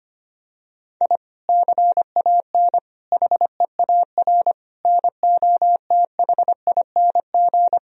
Morse code, heard as a keyed tone, about 25 wpm, 700 Hz. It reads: I CAN HEAR NOTHING
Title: Sentences from Top 1000 Words - Review Part 4 - 25wpm